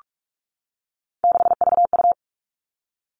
Morse code, heard as a keyed tone, about 44 wpm, 700 Hz. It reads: 6VU
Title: All Cut Numbers - 3-Digit 45wpm